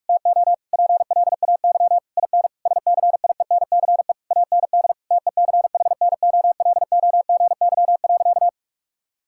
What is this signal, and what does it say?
Morse code, heard as a keyed tone, about 46 wpm, 700 Hz. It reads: TO PLAY IN SCIENCE AND TECHNOLOGY.
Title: Sentences from Top 1000 Words - Review Part 6 - 45wpm